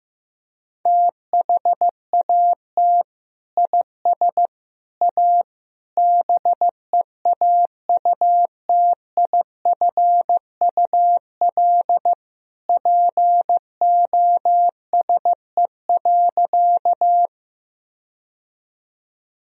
Morse code, heard as a keyed tone, about 14 wpm, 700 Hz. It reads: THAT IS A BEAUTIFUL POSE.
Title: Sentences from Top 1000 Words - Review Part 2 - 15wpm